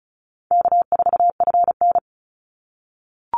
Morse code, heard as a keyed tone, about 34 wpm, 700 Hz. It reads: K4FN
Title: Call Signs - 35wpm